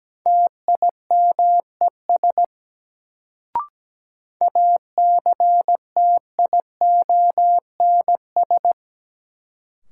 Morse code, heard as a keyed tone, about 17 wpm, 700 Hz. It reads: TIMES ACTIONS
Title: Sentences from Top 1000 Words - Review Part 6 - New Word Drill - 17wpm